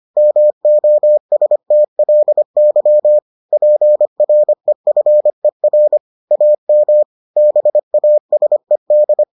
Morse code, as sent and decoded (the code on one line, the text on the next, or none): -- --- ... - .-.. -.-- / .--. .-. ..-. .-. / .- -- / -... .- ... -..
MOSTLY PREFER AM BASED